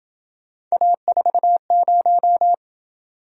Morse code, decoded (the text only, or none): A40